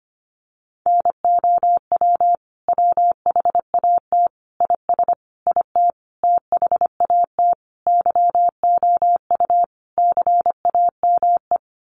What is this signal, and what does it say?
Morse code, heard as a keyed tone, about 25 wpm, 700 Hz. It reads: NOW WHAT IS IT THAT YOU CAME
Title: Sentences from Top 200 Words 25wpm